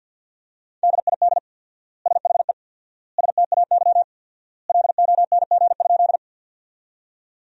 Morse code, heard as a keyed tone, about 47 wpm, 700 Hz. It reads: DID SHE STAY LONG?
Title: 4x Word-Spacing - Sentences from Top 500 Words - 50wpm